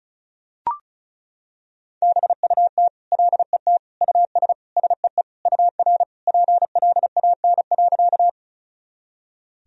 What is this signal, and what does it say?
Morse code, heard as a keyed tone, about 36 wpm, 700 Hz. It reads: BUT LET US SEE UR PLAN.
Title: Sentences from Top 500 Words and Abbreviations 35wpm